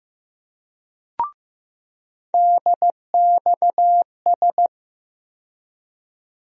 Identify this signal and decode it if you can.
Morse code, heard as a keyed tone, about 15 wpm, 700 Hz. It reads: DXS
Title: USA Call Sign Postfixes - 3 Letters 15wpm